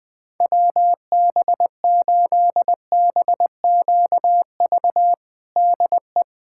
Morse code, heard as a keyed tone, about 20 wpm, 700 Hz. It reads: WB8BQV DE
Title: USA Callsign DE Callsign 20wpm